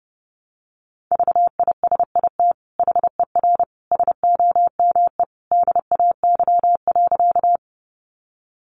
Morse code, heard as a keyed tone, about 30 wpm, 700 Hz. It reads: VISIT HER SOME DAY.
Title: Sentences from Top 900 Words 30wpm